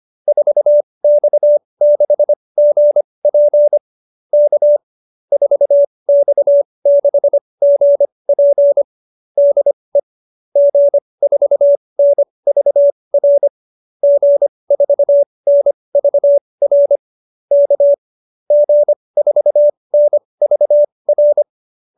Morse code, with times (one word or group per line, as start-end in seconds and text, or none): 0.3-3.8 4X6GP
4.3-4.8 K
5.3-8.8 4X6GP
9.4-10.0 DE
10.6-13.5 G4NVR
14.0-17.0 G4NVR
17.5-17.9 K
18.5-21.4 G4NVR